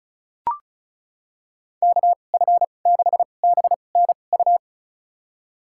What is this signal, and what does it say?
Morse code, heard as a keyed tone, about 36 wpm, spaced as for 25 wpm, 700 Hz. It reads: KF6BNU
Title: USA Call Signs - 6 Characters - 2x Character Spacing - 35wpm